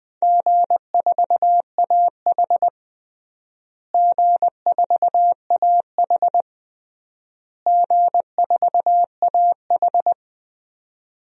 Morse code, heard as a keyed tone, about 20 wpm, 700 Hz. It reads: G4AH G4AH G4AH
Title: DXCC-100 Call Signs - 4 Characters - Repeated 3x - 20wpm